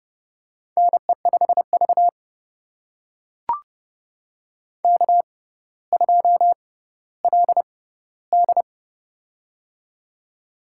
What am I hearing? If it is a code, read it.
Morse code, heard as a keyed tone, about 30 wpm, spaced as for 15 wpm, 700 Hz. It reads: NE5V K2LD
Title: USA Call Signs - 4 Characters - 6x Character Spacing - 30wpm